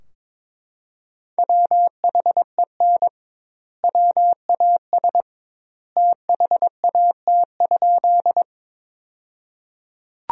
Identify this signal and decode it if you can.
Morse code, heard as a keyed tone, about 22 wpm, 700 Hz. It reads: WHEN WAS THAT?
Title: 2x Word-Spacing - Sentences from Top 100 Words - 22wpm